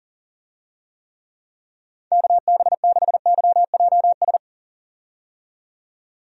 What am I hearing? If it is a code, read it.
Morse code, heard as a keyed tone, about 40 wpm, 700 Hz. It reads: KB6YJS
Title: USA Call Signs - 6 Characters - 40wpm